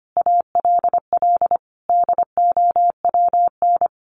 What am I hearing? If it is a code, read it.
Morse code, heard as a keyed tone, about 25 wpm, 700 Hz. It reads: ALL DOWN
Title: A Sound Of Thunder 25wpm